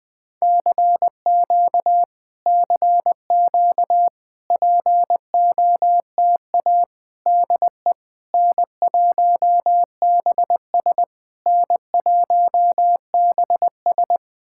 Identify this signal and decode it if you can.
Morse code, heard as a keyed tone, about 20 wpm, 700 Hz. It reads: CQ CQ POTA DE N1BS N1BS
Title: POTA QSO Practice - 20wpm